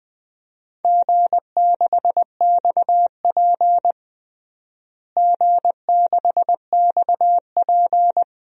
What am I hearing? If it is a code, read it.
Morse code, heard as a keyed tone, about 20 wpm, 700 Hz. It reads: G6XP G6XP
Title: DXCC-100 Call Signs - 4 Characters - Repeated 3x - 20wpm